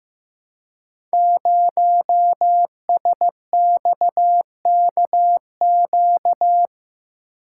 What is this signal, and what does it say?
Morse code, heard as a keyed tone, about 15 wpm, 700 Hz. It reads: 0SXKQ
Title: Code Groups - 5 Letter:Numbers - 15wpm